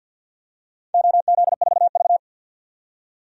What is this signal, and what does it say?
Morse code, heard as a keyed tone, about 51 wpm, 700 Hz. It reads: OZ4V
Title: DXCC-100 Call Signs - 4 Characters - 50wpm